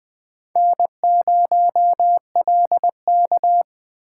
Morse code, heard as a keyed tone, about 20 wpm, 700 Hz. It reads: N0LK
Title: USA Call Signs - 4 Characters - 4x Character Spacing - 20wpm